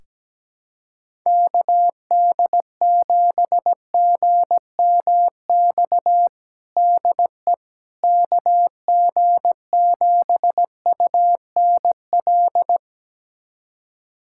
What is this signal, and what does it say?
Morse code, heard as a keyed tone, about 17 wpm, 700 Hz. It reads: KD7GMX DE KG7UNL
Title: USA Callsign DE Callsign - 6 Characters - 17wpm